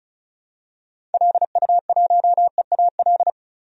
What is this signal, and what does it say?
Morse code, heard as a keyed tone, about 34 wpm, 700 Hz. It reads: LU1EAL
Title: DXCC-100 Call Signs - 6 Characters - 35wpm